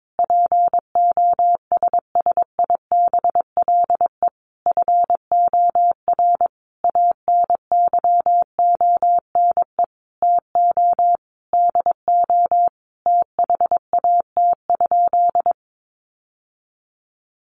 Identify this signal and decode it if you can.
Morse code, heard as a keyed tone, about 22 wpm, 700 Hz. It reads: POSSIBLE FOR ANYONE TO DO THAT?
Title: Sentences from Top 900 Words 22wpm